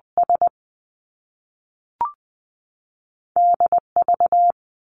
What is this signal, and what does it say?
Morse code, heard as a keyed tone, about 20 wpm, spaced as for 5 wpm, 700 Hz. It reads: SDV
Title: Two Challenging Letter-Number Combinations 20wpm